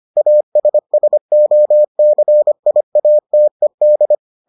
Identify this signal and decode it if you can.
Morse code, heard as a keyed tone, about 25 wpm, 600 Hz. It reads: ASSOCIATED